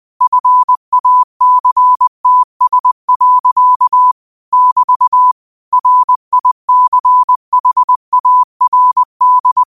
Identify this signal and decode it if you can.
Morse code, heard as a keyed tone, about 20 wpm, 1000 Hz. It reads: FACTS. <BT> RICHARD
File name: qotdfiles / 20wpm